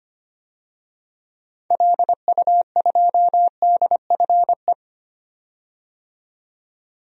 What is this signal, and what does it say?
Morse code, heard as a keyed tone, about 25 wpm, 700 Hz. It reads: LU2DFE
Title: DXCC-100 Call Signs - 6 Characters - 25wpm